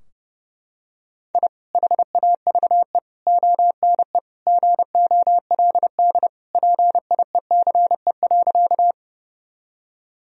Morse code, heard as a keyed tone, about 32 wpm, 700 Hz. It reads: I HAVE ONE GOLD PIECE.
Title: Sentences from Top 1000 Words - Review Part 3 - 30wpm